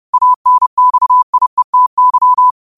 Morse code, heard as a keyed tone, about 30 wpm, 1000 Hz. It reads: ANXIETY
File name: qotdfiles / 30wpm